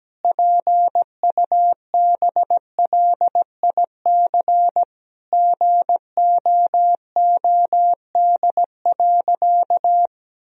Morse code, heard as a keyed tone, about 17 wpm, 700 Hz. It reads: PUBLIC GOOD.
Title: Sentences from Top 1000 Words - Review Part 3 - 17wpm